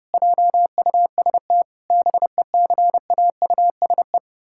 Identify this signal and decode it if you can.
Morse code, heard as a keyed tone, about 30 wpm, 700 Hz. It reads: JUST BECAUSE
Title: Pangram Challenge v2 - 30wpm